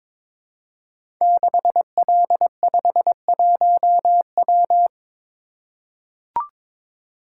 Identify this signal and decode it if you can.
Morse code, heard as a keyed tone, about 22 wpm, 700 Hz. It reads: 6L51W
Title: Code Groups - 5 Letter:Numbers - 22wpm